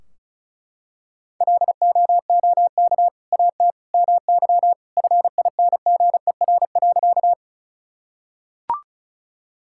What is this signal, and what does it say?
Morse code, heard as a keyed tone, about 36 wpm, 700 Hz. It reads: LOOK AT MY FINGER.
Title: Sentences from Top 1600 Words 35wpm